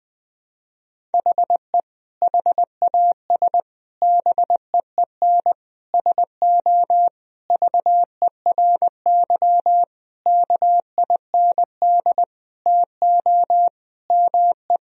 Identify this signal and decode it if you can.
Morse code, heard as a keyed tone, about 20 wpm, 700 Hz. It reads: HE HAS BEEN SO VERY KIND TO ME
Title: Sentences from Top 1000 Words - Review Part 1 - 20wpm